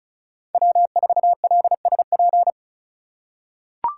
Morse code, heard as a keyed tone, about 36 wpm, 700 Hz. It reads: W4LSP
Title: USA Call Signs - 2x Character Spacing - 35wpm